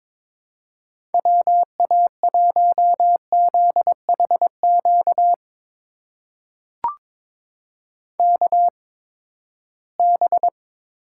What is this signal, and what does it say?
Morse code, heard as a keyed tone, about 22 wpm, spaced as for 15 wpm, 700 Hz. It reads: WA1ZHQ KB
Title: USA Call Signs - 6 Characters - 8x Character Spacing - 22wpm